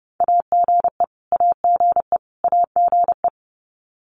Morse code, heard as a keyed tone, about 30 wpm, 700 Hz. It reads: AGE AGE AGE
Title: Top 500 Words Repeated 30wpm